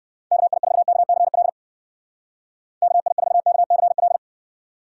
Morse code, heard as a keyed tone, about 58 wpm, 700 Hz. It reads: KI4KCL KI4KCL